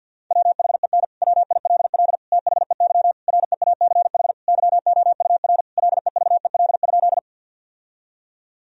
Morse code, heard as a keyed tone, about 50 wpm, 700 Hz. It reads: WHEN WILL THEY REACH YOUR LEVEL?
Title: Sentences from Top 1000 Words - Review Part 4 - 50wpm